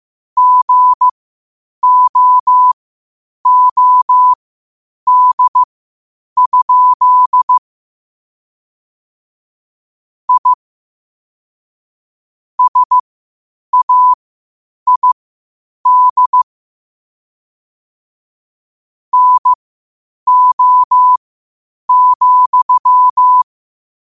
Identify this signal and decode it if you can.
Morse code, heard as a keyed tone, about 15 wpm, spaced as for 8 wpm, 1000 Hz. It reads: GOOD? I SAID NO,
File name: qotdfiles / 8wpm